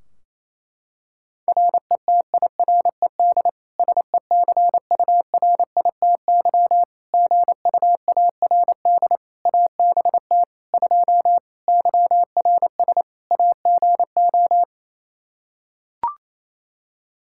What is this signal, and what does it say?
Morse code, heard as a keyed tone, about 28 wpm, 700 Hz. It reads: RETIRED SECURITY GUARD ABT 2 YRS AGO
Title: Practice QSO - Exchange 3 - More Info - 28wpm